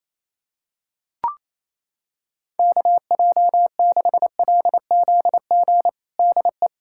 Morse code, heard as a keyed tone, about 28 wpm, 700 Hz. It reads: KJ6LZG DE
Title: USA Callsign DE Callsign - 6 Characters - 28wpm